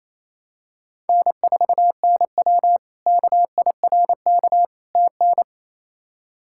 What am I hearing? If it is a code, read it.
Morse code, heard as a keyed tone, about 28 wpm, 700 Hz. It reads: N4NW KIRK TN
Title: K1USN SST QSO Practice - 28wpm